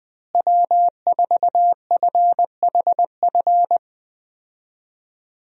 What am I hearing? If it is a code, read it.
Morse code, heard as a keyed tone, about 21 wpm, 700 Hz. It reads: W4FHF
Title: USA Call Signs - 5 Characters - Repeated 3x - 20wpm